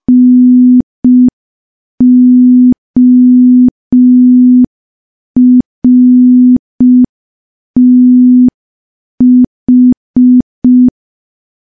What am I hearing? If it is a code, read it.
Morse code, heard as a keyed tone, about 5 wpm, 250 Hz. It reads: NORTH
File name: morse / morse-training-05WPM